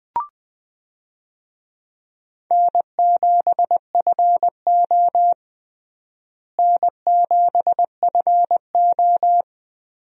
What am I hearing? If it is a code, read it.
Morse code, heard as a keyed tone, about 21 wpm, 700 Hz. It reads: N7FO N7FO
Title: USA Call Signs - 4 Characters - Repeated 2x - 20wpm